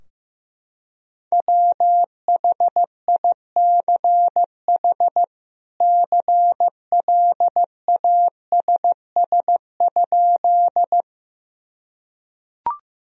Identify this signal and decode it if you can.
Morse code, heard as a keyed tone, about 16 wpm, 700 Hz. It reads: WHICH CLASS?
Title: Sentences from Top 1000 Words - Review Part 2 - 15wpm